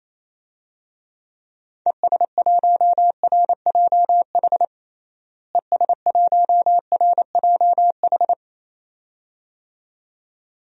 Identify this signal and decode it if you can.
Morse code, heard as a keyed tone, about 28 wpm, 700 Hz. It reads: ES1RJH ES1RJH
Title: DXCC-100 Call Signs - 6 Characters - Repeated 2x - 28wpm